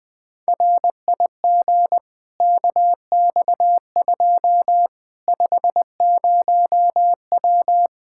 Morse code, heard as a keyed tone, about 20 wpm, 700 Hz. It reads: RIG KX2 50W
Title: Practice QSO - Exchange 2 - Info - 20wpm